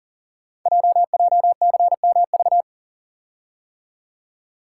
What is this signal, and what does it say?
Morse code, heard as a keyed tone, about 43 wpm, 700 Hz. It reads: JJCMV
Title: Code Groups - 5 Letters - 40wpm